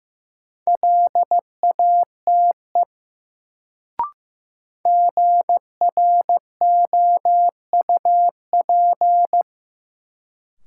Morse code, heard as a keyed tone, about 15 wpm, 700 Hz. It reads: LATE GROUP
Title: Sets of 3 Words - New Word Drill - 15wpm